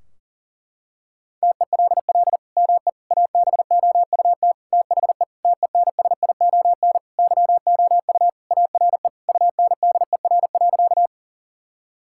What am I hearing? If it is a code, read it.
Morse code, heard as a keyed tone, about 40 wpm, 700 Hz. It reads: TELL ME ABOUT THE TENSION YOU ARE UNDER.
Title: Sentences from Top 1600 Words 40wpm